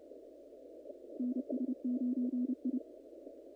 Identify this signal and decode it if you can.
Morse code, heard as a keyed tone, about 29 wpm, 220 Hz. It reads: NS9I
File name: contests / naqp-cw-k5tr-op-k5pi- -2100